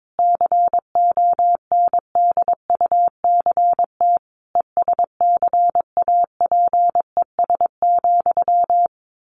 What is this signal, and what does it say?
Morse code, heard as a keyed tone, about 22 wpm, 700 Hz. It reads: CONDUCT ESCAPES,
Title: A Sound Of Thunder 22wpm